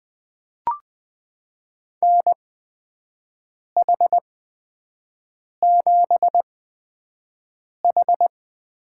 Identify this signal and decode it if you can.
Morse code, heard as a keyed tone, about 20 wpm, spaced as for 5 wpm, 700 Hz. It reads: NH7H
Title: USA Call Signs - 4 Characters - 8x Character Spacing - 20wpm